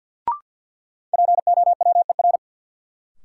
Morse code, heard as a keyed tone, about 52 wpm, 700 Hz. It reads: POWER
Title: Sentences from Top 500 Words - New Word Drill - 50wpm